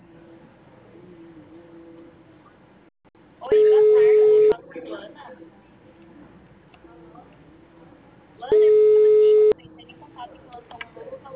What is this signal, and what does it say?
Ringback tone (Europe, 425 Hz), with speech.
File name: out- -2015- - - .87